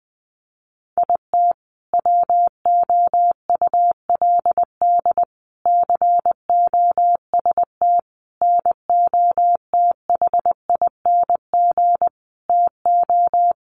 Morse code, heard as a keyed tone, about 19 wpm, 700 Hz. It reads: IT WOULD COST NOTHING TO
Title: Sentences from Top 800 Words 20wpm